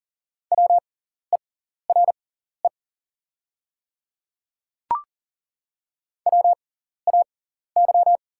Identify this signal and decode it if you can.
Morse code, heard as a keyed tone, about 40 wpm, spaced as for 14 wpm, 690 Hz. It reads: WERE WAY
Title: 6x Character Spacing - Top 100 Words - 40wpm